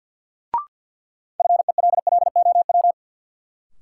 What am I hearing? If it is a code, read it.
Morse code, heard as a keyed tone, about 46 wpm, 700 Hz. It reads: FELLOW
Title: Sets of 5 Words Encore - New Word Drill - 50wpm